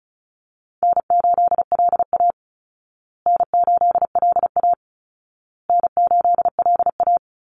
Morse code, heard as a keyed tone, about 34 wpm, 700 Hz. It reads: N8LA N8LA N8LA
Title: Easy Call Signs Repeated 3x - 35wpm